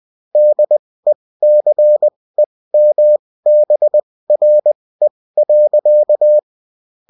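Morse code, heard as a keyed tone, about 20 wpm, spaced as for 16 wpm, 600 Hz. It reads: DECEMBRE.
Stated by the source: Bulletin in text and CW form.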